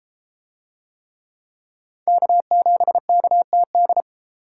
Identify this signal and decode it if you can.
Morse code, heard as a keyed tone, about 32 wpm, 700 Hz. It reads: K7KTD